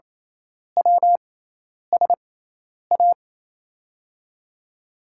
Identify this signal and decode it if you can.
Morse code, heard as a keyed tone, about 28 wpm, spaced as for 10 wpm, 700 Hz. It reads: WSA
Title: USA Call Signs - 6x Character Spacing - 28wpm